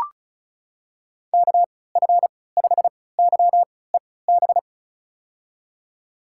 Morse code, heard as a keyed tone, about 36 wpm, spaced as for 20 wpm, 700 Hz. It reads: KF5YEB
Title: USA Call Signs - 6 Characters - 3x Character Spacing - 35wpm